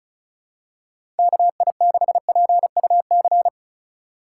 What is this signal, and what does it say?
Morse code, heard as a keyed tone, about 36 wpm, 700 Hz. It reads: KI6PUC
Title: USA Call Signs - 3x Character Spacing - 35wpm